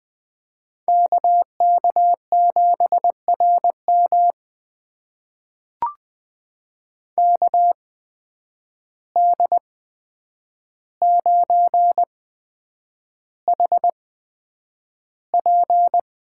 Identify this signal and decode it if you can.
Morse code, heard as a keyed tone, about 20 wpm, spaced as for 9 wpm, 700 Hz. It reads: KK7RM KD9HP
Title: USA Call Signs - 8x Character Spacing - 20wpm